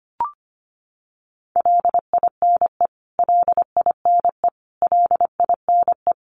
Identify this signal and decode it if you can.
Morse code, heard as a keyed tone, about 25 wpm, 700 Hz. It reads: LINE LINE LINE
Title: Top 200 Words Repeated 25wpm